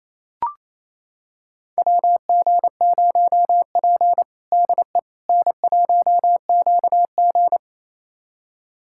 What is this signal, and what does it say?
Morse code, heard as a keyed tone, about 28 wpm, 700 Hz. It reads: WG0P DE N1QG
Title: USA Callsign DE Callsign - 4 Characters - 28wpm